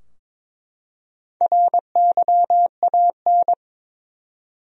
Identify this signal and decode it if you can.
Morse code, heard as a keyed tone, about 22 wpm, 700 Hz. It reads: RYAN
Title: USA Common Names - 22wpm